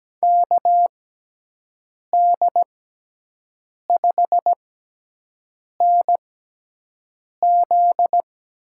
Morse code, heard as a keyed tone, about 17 wpm, spaced as for 6 wpm, 700 Hz. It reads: KD5NZ
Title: USA Call Signs - 6 Characters - 6x Character Spacing - 17wpm